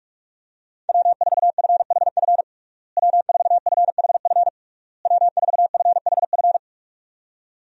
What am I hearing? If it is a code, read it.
Morse code, heard as a keyed tone, about 46 wpm, 700 Hz. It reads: W4FHF W4FHF W4FHF
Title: USA Call Signs - 5 Characters - Repeated 3x - 45wpm